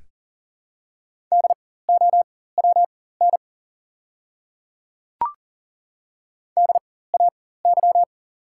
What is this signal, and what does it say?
Morse code, heard as a keyed tone, about 40 wpm, spaced as for 19 wpm, 700 Hz. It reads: DOWN DAY
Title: 4x Character Spacing - Top 100 Words - 40wpm